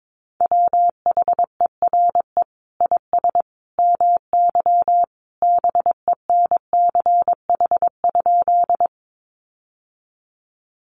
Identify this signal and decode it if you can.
Morse code, heard as a keyed tone, about 22 wpm, 700 Hz. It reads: WHERE IS MY BENCH?
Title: Sentences from Top 1400 Words 22wpm